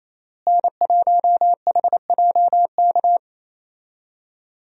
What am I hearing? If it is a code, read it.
Morse code, heard as a keyed tone, about 27 wpm, 700 Hz. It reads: N1HJK
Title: USA Call Signs - 5 Characters - Repeated 2x - 28wpm